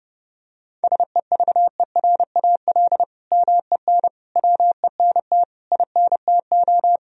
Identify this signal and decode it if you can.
Morse code, heard as a keyed tone, about 30 wpm, 700 Hz. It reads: SEVERAL MEN WENT INTO
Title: Sentences from Top 1000 Words - Review Part 2 - 30wpm